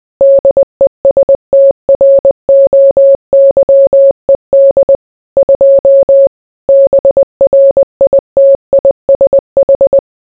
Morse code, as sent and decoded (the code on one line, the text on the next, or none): -.. ... - .-. --- -.-- . -.. / ..--- / -... .-. .. - .. ... ....
DESTROYED 2 BRITISH